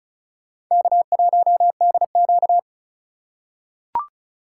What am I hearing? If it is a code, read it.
Morse code, heard as a keyed tone, about 34 wpm, 700 Hz. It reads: K1DQ
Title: USA Call Signs - 4 Characters - 2x Character Spacing - 35wpm